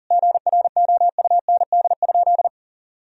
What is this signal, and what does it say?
Morse code, heard as a keyed tone, about 40 wpm, 700 Hz. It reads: GROUND?
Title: Sentences from Top 1000 Words - Review Part 3 - 40wpm